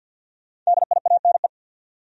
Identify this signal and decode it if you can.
Morse code, heard as a keyed tone, about 50 wpm, 700 Hz. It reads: DIANE
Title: USA Common Names - 50wpm